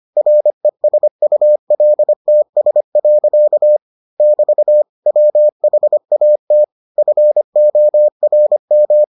Morse code, heard as a keyed tone, about 25 wpm, 600 Hz. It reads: RESULTS. <BT> WHAT FORM